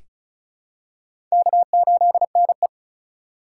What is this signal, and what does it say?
Morse code, heard as a keyed tone, about 36 wpm, 690 Hz. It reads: K8NE
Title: USA Call Signs - 4 Characters - 2x Character Spacing - 35wpm